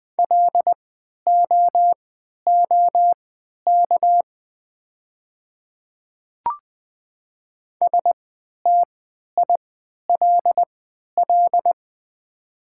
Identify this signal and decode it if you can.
Morse code, heard as a keyed tone, about 20 wpm, spaced as for 11 wpm, 700 Hz. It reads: LOOK STILL
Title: 3x Character Spacing - Top 300 Words - 20wpm